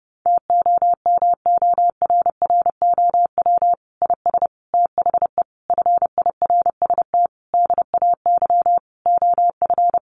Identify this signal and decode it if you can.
Morse code, heard as a keyed tone, about 30 wpm, 700 Hz. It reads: TOMORROW IS THE FIRST DAY OF
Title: Additional Easy Phrases 30wpm